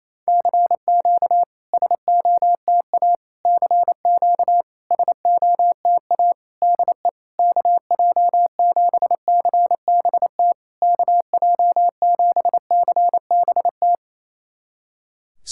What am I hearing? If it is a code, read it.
Morse code, heard as a keyed tone, about 27 wpm, 700 Hz. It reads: CQ SOTA CQ SOTA DE KJ7CBT KJ7CBT
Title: SOTA QSO Practice 28wpm